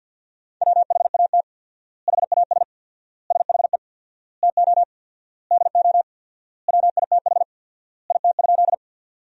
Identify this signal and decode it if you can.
Morse code, heard as a keyed tone, about 47 wpm, 700 Hz. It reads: WHAT HAS SHE TO DO WITH IT?
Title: 4x Word-Spacing - Sentences from Top 100 Words - 50wpm